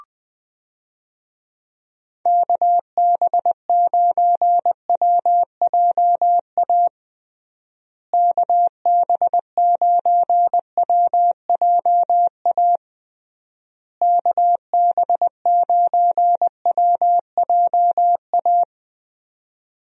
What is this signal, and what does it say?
Morse code, heard as a keyed tone, about 20 wpm, 700 Hz. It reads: KB9WJA KB9WJA KB9WJA
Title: USA Call Signs - 6 Characters - Repeated 3x - 20wpm